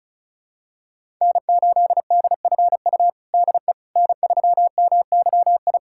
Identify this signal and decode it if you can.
Morse code, heard as a keyed tone, about 36 wpm, 700 Hz. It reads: N8DFU DE N3MYI
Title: USA Callsign DE Callsign - 5 Characters - 35wpm